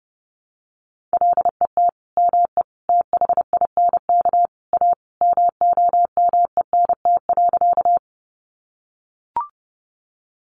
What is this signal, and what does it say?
Morse code, heard as a keyed tone, about 30 wpm, 700 Hz. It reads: LET ME THINK A MOMENT.
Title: Sentences from Top 1000 Words 30wpm